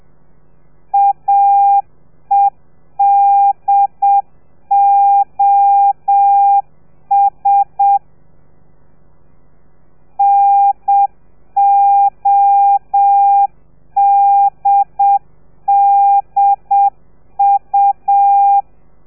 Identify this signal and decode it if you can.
Morse code, heard as a keyed tone, about 7 wpm, 800 Hz. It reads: AEDOS NODDU